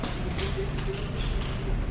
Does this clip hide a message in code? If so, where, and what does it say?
No.